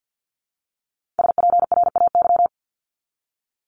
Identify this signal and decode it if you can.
Morse code, heard as a keyed tone, about 50 wpm, 700 Hz. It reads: SPRAY
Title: Multi-letter Phonemes Repeated 3x 50wpm